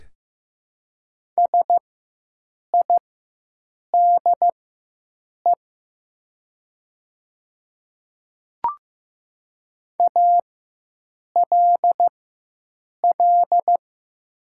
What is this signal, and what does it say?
Morse code, heard as a keyed tone, about 15 wpm, spaced as for 7 wpm, 700 Hz. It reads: SIDE ALL
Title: 4x Character Spacing - Top 100 Words - 15wpm